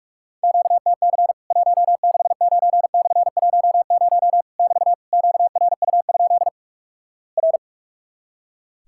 Morse code, heard as a keyed tone, about 46 wpm, 700 Hz. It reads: QTC 169/10 <BT> QRU? R
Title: Worked All Europe DX Contest - QTC Exchange - 45wpm